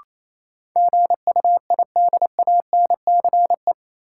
Morse code, heard as a keyed tone, about 28 wpm, 700 Hz. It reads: GUIDANCE
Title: Sentences from Top 1600 Words - New Word Drill - 28wpm